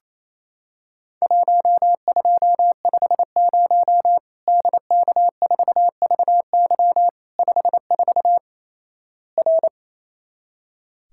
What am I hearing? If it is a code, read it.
Morse code, heard as a keyed tone, about 28 wpm, 700 Hz. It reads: 1250 DK4VY 54 R
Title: Worked All Europe DX Contest - QTC Exchange - 28wpm